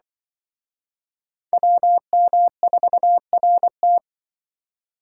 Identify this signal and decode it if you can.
Morse code, heard as a keyed tone, about 25 wpm, 700 Hz. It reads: WM4RT
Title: Speed Racing - USA 2x2 Call Signs 20wpm